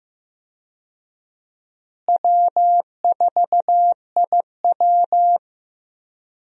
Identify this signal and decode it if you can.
Morse code, heard as a keyed tone, about 15 wpm, 700 Hz. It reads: W4IW
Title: USA Call Signs - 4 Characters - Repeated 2x - 15wpm